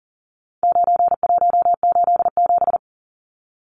Morse code, heard as a keyed tone, about 40 wpm, 700 Hz. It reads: 9187
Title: Numbers - 4-Digit 40wpm